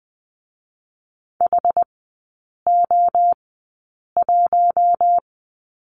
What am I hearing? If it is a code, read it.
Morse code, heard as a keyed tone, about 20 wpm, 700 Hz. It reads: H O 1
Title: 114 Lesson 19e - Letter w - Three Characters 20wpm